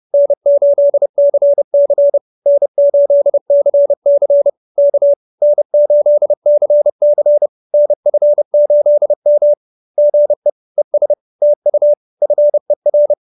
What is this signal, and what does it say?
Morse code, heard as a keyed tone, about 30 wpm, 590 Hz. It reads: N8CC N8CC K N8CC NF8M GE ES TU FER
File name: QSO321 30